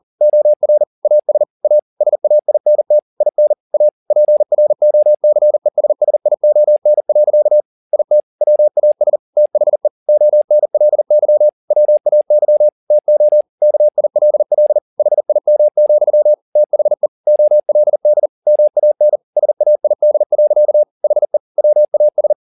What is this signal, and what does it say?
Morse code, heard as a keyed tone, about 40 wpm, 600 Hz. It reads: OR AS A SAINT IN A PROCESSION. IT WAS THE ONLY WAY TO KILL HIM, THE OLD MAN SAID. HE WAS
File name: the old man and the sea 2 40wpm